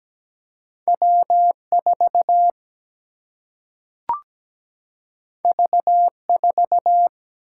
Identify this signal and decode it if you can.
Morse code, heard as a keyed tone, about 17 wpm, 700 Hz. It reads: W4 V4
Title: Single-Letter Single-Number 17wpm